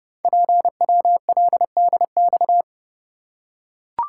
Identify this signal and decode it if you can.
Morse code, heard as a keyed tone, about 30 wpm, 700 Hz. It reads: PWLDX
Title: Code Groups - 5 Letter:Numbers - 30wpm